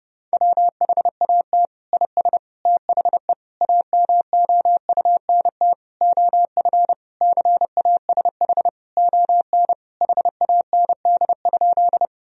Morse code, heard as a keyed tone, about 30 wpm, 700 Hz. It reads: WHAT IS THE AMOUNT OF CASH ON HAND?
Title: Sentences from Top 1000 Words - Review Part 4 - 30wpm